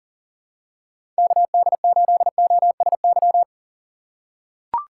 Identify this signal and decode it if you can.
Morse code, heard as a keyed tone, about 40 wpm, 700 Hz. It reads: KD8OSY
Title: USA Call Signs - 6 Characters - 2x Character Spacing - 40wpm